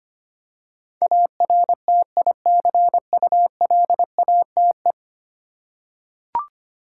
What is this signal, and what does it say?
Morse code, heard as a keyed tone, about 25 wpm, 700 Hz. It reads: ARTICULATE
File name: 10 Letter Words - First 3 Letters Given - 25wpm